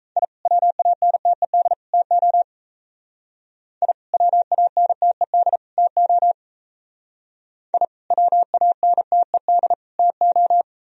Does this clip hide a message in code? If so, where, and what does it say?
Yes: Morse code from 0.2 to 10.6 s, heard as a keyed tone, about 36 wpm, 700 Hz. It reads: I WANTED TO I WANTED TO I WANTED TO